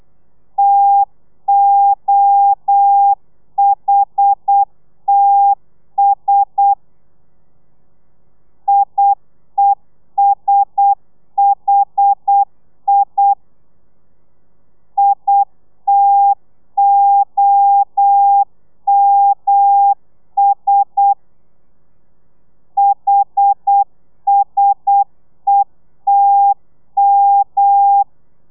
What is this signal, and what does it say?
Morse code, heard as a keyed tone, about 8 wpm, 800 Hz. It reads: TOHTS IESHI ITOMS HSETM
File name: telegrafia / i2tfj / Lez1 35